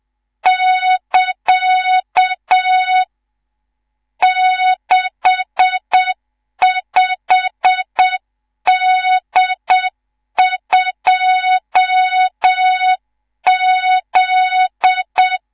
Morse code, heard as a keyed tone, about 7 wpm, 3700 Hz, with noise. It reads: <KA> 65D2Z
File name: Morze-HA2MN / hang / 19 GYAKORLAT